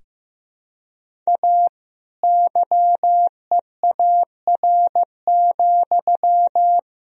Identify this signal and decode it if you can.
Morse code, heard as a keyed tone, about 15 wpm, 700 Hz. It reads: A YEAR,
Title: Sentences from Top 1000 Words - Review Part 1 - 15wpm